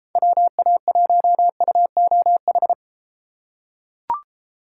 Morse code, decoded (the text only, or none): WA1UOH